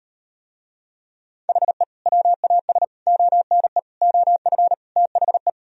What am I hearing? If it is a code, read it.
Morse code, heard as a keyed tone, about 38 wpm, 700 Hz. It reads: HE WAS ONE OF THE